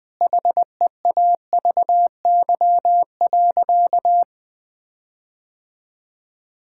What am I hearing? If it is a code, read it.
Morse code, heard as a keyed tone, about 20 wpm, 700 Hz. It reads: HEAVY.
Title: Sentences from Top 1600 Words 20wpm